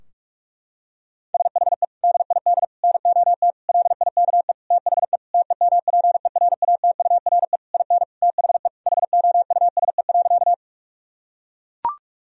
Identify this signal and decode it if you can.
Morse code, heard as a keyed tone, about 41 wpm, 700 Hz. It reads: SHE DID NOT LIKE THE TEMPERATURE IN THE HOUSE.
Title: Sentences from Top 1000 Words - Review Part 4 - 45wpm